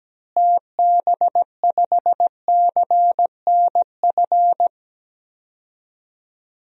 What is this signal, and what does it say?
Morse code, heard as a keyed tone, about 17 wpm, 700 Hz. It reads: TB5CNF
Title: DXCC-100 Call Signs - 6 Characters - Repeated 2x - 17wpm